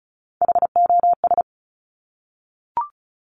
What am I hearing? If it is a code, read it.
Morse code, heard as a keyed tone, about 36 wpm, 700 Hz. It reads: HOS
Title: Top Three Letter Combinations in Words 35wpm